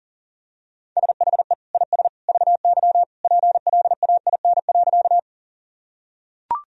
Morse code, heard as a keyed tone, about 37 wpm, 700 Hz. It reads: SHE IS VY PLAIN.
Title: Sentences from Top 400 Words and Abbreviations 40wpm